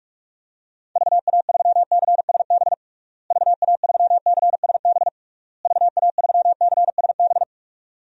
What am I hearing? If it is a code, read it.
Morse code, heard as a keyed tone, about 46 wpm, 700 Hz. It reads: VA3CSB VA3CSB VA3CSB
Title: DXCC-100 Call Signs - 6 Characters - Repeated 3x - 45wpm